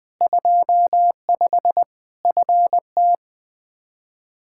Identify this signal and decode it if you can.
Morse code, heard as a keyed tone, about 20 wpm, 700 Hz. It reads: 25 FT
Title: K1USN SST QSO Practice 20wpm